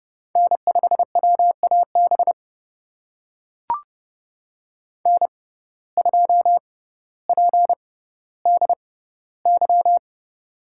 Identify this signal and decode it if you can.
Morse code, heard as a keyed tone, about 29 wpm, spaced as for 15 wpm, 700 Hz. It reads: N5WAB N2PDY
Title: USA Call Signs - 5 Characters - 6x Character Spacing - 30wpm